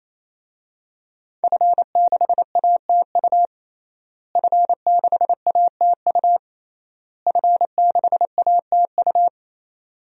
Morse code, heard as a keyed tone, about 27 wpm, 700 Hz. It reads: F6ATU F6ATU F6ATU
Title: DXCC-100 Call Signs - 5 Characters - Repeated 3x - 28wpm